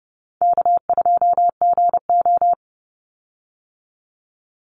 Morse code, heard as a keyed tone, about 30 wpm, 700 Hz. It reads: K2GO
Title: Call Signs Repeated 3x - 30wpm